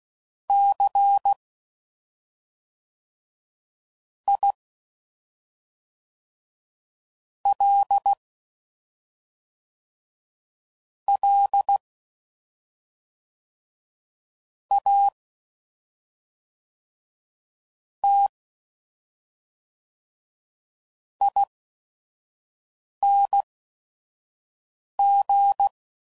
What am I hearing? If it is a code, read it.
Morse code, heard as a keyed tone, about 16 wpm, spaced as for 5 wpm, 800 Hz. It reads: C I L L A T ING
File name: Suffixes / Special ING's / SlowToFaster ING's